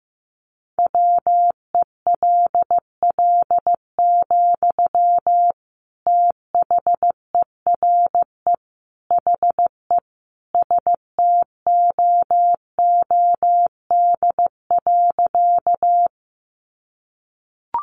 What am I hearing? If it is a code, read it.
Morse code, heard as a keyed tone, about 15 wpm, 700 Hz. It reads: WELL, THERE HE STOOD.
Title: Sentences from Top 500 Words 15wpm